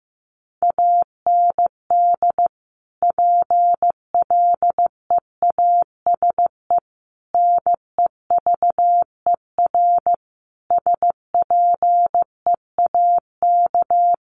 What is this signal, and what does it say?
Morse code, heard as a keyed tone, about 14 wpm, 700 Hz. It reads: AND PLEASE NEVER SPEAK
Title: Sentences from Top 1500 Words 15wpm